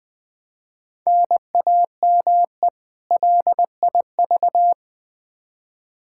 Morse code, heard as a keyed tone, about 20 wpm, 700 Hz. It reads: NAME LIV
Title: Practice QSO - Exchange 1 - Beginning - 20wpm